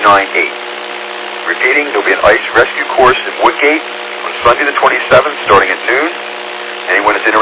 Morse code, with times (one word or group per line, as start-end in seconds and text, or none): none